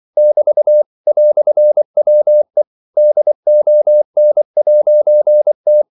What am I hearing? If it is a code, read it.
Morse code, heard as a keyed tone, about 24 wpm, 600 Hz. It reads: <BT> "WE DON'T
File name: CW / Practice / EnglishQuotations0424